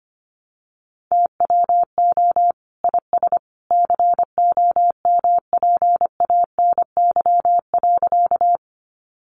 Morse code, heard as a keyed tone, about 25 wpm, 700 Hz. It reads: TWO IS COMPANY.
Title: Sentences from Top 600 Words 25wpm